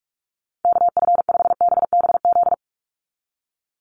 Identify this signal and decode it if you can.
Morse code, heard as a keyed tone, about 44 wpm, 700 Hz. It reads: KF5BBZ
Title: USA Call Signs 45wpm